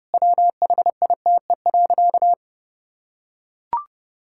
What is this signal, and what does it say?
Morse code, heard as a keyed tone, about 30 wpm, 700 Hz. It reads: WHITE.
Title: Sentences from Top 1600 Words 30wpm